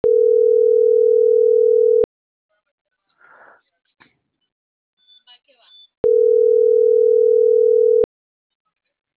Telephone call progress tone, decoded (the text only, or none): ringback tone